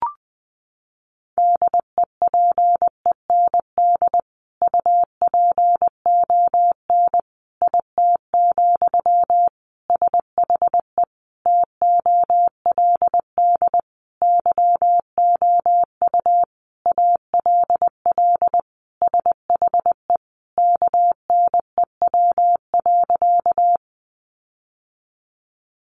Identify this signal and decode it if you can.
Morse code, heard as a keyed tone, about 20 wpm, 700 Hz. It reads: DEPEND UPON IT, SHE TOLD YOU ALL SHE KNEW.
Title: Sentences from Top 1300 Words 20wpm